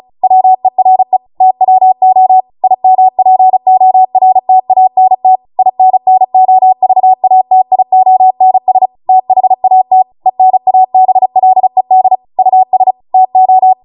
Morse code, heard as a keyed tone, about 36 wpm, 750 Hz. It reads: WERE TWO IMPORTANT INNOVATIONS THAT ENABLED US TO